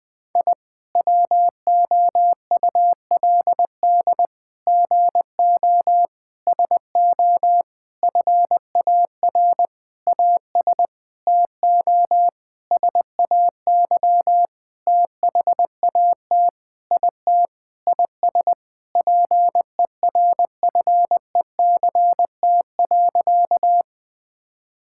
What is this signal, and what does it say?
Morse code, heard as a keyed tone, about 20 wpm, 700 Hz. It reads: I WOULD GO SO FAR AS TO SAY THAT IT IS PERFECT.
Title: Sentences with 9-Word Phrases - Spoken Beginning - 20wpm